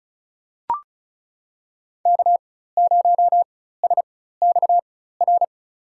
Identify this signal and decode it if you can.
Morse code, heard as a keyed tone, about 36 wpm, spaced as for 16 wpm, 700 Hz. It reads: K0SXR
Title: USA Call Signs - 5 Characters - 4x Character Spacing - 35wpm